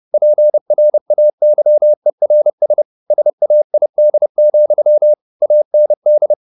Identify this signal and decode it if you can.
Morse code, heard as a keyed tone, about 30 wpm, 600 Hz. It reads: PRAYERS SAID, AND
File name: the old man and the sea / the old man and the sea 1 30wpm